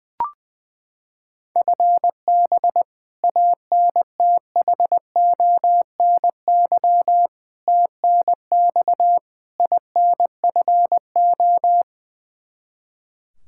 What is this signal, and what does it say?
Morse code, heard as a keyed tone, about 20 wpm, 700 Hz. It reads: FB ANTHONY TNX INFO
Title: Practice QSO - Exchange 3 - More Info - 20wpm